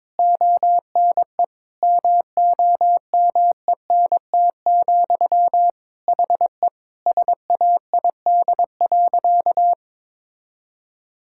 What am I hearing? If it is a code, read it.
Morse code, heard as a keyed tone, about 22 wpm, 700 Hz. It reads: ONE MOMENT, HE SAID.
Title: Sentences from Top 1000 Words - Review Part 5 - 22wpm